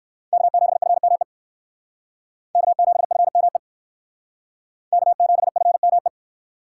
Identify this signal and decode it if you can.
Morse code, heard as a keyed tone, about 58 wpm, 700 Hz. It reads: K7FME K7FME K7FME